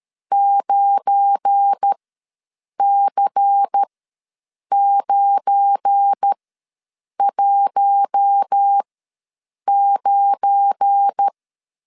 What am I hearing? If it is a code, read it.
Morse code, heard as a keyed tone, about 13 wpm, spaced as for 7 wpm, 800 Hz. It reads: 9C919
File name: LEZIONI / 7 GF9Z / 7 GF9 ws 12wpm 3344